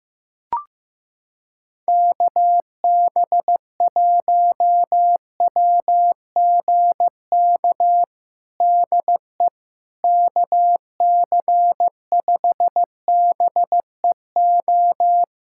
Morse code, heard as a keyed tone, about 16 wpm, 700 Hz. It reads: KB1WGK DE KC5BEO
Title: USA Callsign DE Callsign - 6 Characters - 15wpm